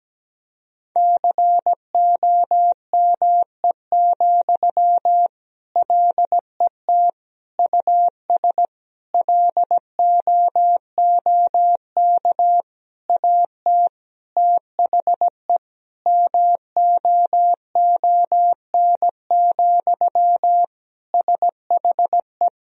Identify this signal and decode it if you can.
Morse code, heard as a keyed tone, about 17 wpm, 700 Hz. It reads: COME, LET US LOOK AT THE MOON, SHE
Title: Sentences from Top 1000 Words - Review Part 3 - 17wpm